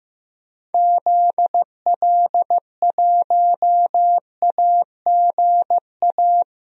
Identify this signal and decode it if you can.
Morse code, heard as a keyed tone, about 15 wpm, 700 Hz. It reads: ZL1AGA
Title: DXCC-100 Call Signs - 6 Characters - Repeated 3x - 15wpm